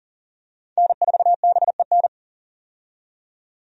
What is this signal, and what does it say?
Morse code, heard as a keyed tone, about 40 wpm, 700 Hz. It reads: N4BEN
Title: USA Call Signs - 5 Characters - 40wpm